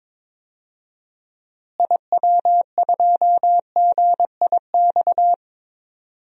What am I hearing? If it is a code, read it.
Morse code, heard as a keyed tone, about 22 wpm, 700 Hz. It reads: IW2GIX
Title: DXCC-100 Call Signs - 6 Characters - Repeated 2x - 22wpm